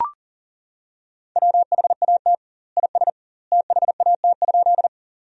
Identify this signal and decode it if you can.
Morse code, heard as a keyed tone, about 40 wpm, 700 Hz. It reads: WHAT IS THAT?
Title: 2x Word-Spacing - Sentences from Top 100 Words - 40wpm